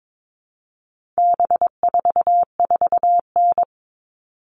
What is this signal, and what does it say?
Morse code, heard as a keyed tone, about 22 wpm, 700 Hz. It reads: B44N
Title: All Cut Numbers - 4-Digit 22wpm